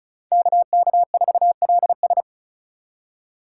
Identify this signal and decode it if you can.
Morse code, heard as a keyed tone, about 36 wpm, 700 Hz. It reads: KK4LS
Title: USA Call Signs - 8x Character Spacing - 35wpm